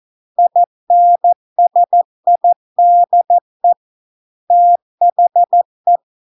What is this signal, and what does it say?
Morse code, heard as a keyed tone, about 14 wpm, 700 Hz. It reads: INSIDE THE